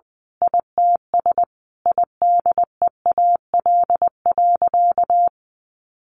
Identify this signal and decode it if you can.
Morse code, heard as a keyed tone, about 19 wpm, 700 Hz. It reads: ITS IDEAL.
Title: Sentences from Top 1400 Words 20wpm